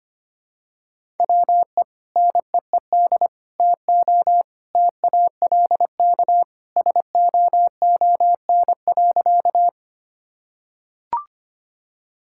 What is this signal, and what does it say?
Morse code, heard as a keyed tone, about 25 wpm, 700 Hz. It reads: WE NEED TO TALK SOON.
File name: practice / 25wpm / Sentences with 2-Word Phrases - 25wpm